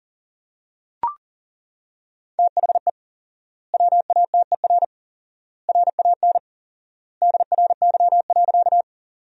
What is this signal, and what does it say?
Morse code, heard as a keyed tone, about 40 wpm, 700 Hz. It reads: THE WATER RAN DRY.
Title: 4x Word-Spacing - Sentences from Top 500 Words - 40wpm